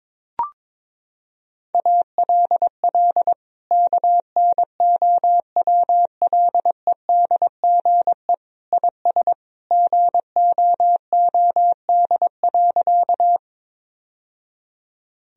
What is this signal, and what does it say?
Morse code, heard as a keyed tone, about 22 wpm, 700 Hz. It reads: ALL KNOWLEDGE IS GOOD.
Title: Sentences from Top 1000 Words - Review Part 3 - 22wpm